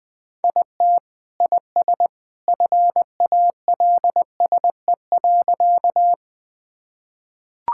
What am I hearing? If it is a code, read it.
Morse code, heard as a keyed tone, about 20 wpm, 700 Hz. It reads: IT IS FALSE.
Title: Sentences from Top 1600 Words 20wpm